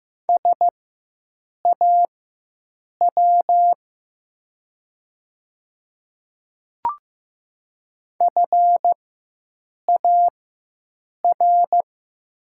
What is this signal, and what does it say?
Morse code, heard as a keyed tone, about 15 wpm, spaced as for 7 wpm, 700 Hz. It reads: SAW FAR
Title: 4x Character Spacing - Top 500 Words - 15wpm